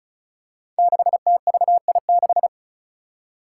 Morse code, heard as a keyed tone, about 36 wpm, 700 Hz. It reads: 6TVI6
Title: Five Challenging Letter-Number Combinations 35wpm